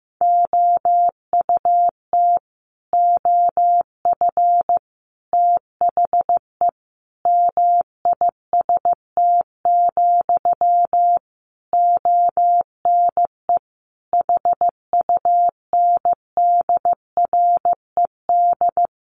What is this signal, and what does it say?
Morse code, heard as a keyed tone, about 15 wpm, 700 Hz. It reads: OUT OF THE MIST, ONE HUNDRED
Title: A Sound Of Thunder 15wpm